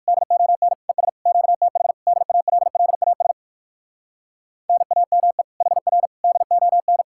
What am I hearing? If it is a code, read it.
Morse code, heard as a keyed tone, about 59 wpm, 700 Hz. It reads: DON ES QTH DALLAS NAME HR DON